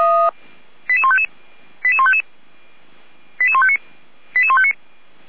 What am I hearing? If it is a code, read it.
Five-tone selective call (ZVEI), heard as quick runs of five tones, tones of about 70 ms. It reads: 88155 88155 88159 88159